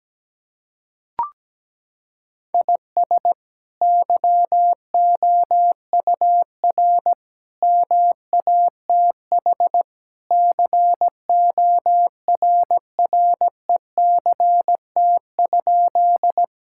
Morse code, heard as a keyed tone, about 17 wpm, 700 Hz. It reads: IS YOUR MATH CORRECT?
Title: Sentences from Top 1000 Words - Review Part 6 - 17wpm